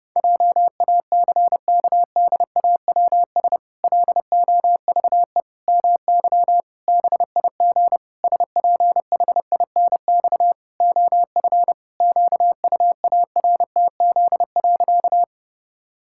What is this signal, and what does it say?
Morse code, heard as a keyed tone, about 32 wpm, 700 Hz. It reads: JACKDAWS LOVE MY BIG SPHINX OF QUARTZ.
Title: Pangram Challenge v2 - 30wpm